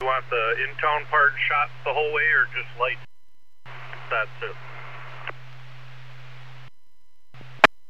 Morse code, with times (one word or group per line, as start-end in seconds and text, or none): none